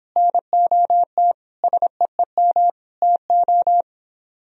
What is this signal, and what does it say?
Morse code, heard as a keyed tone, about 29 wpm, 700 Hz. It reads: NOT SEEM TO